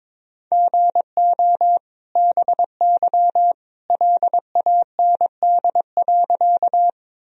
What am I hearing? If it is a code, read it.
Morse code, heard as a keyed tone, about 21 wpm, 700 Hz. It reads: GO BY LAND.
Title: Sentences from Top 1000 Words - Review Part 1 - 22wpm